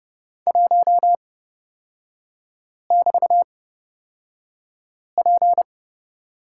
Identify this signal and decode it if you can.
Morse code, heard as a keyed tone, about 30 wpm, spaced as for 5 wpm, 700 Hz. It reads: 1<BT>P
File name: Lesson 36e - BT - Three Characters 30wpm